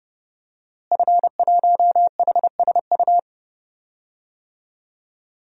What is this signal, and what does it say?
Morse code, heard as a keyed tone, about 32 wpm, 700 Hz. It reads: F1HSU
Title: DXCC-100 Call Signs - 5 Characters - Repeated 2x - 30wpm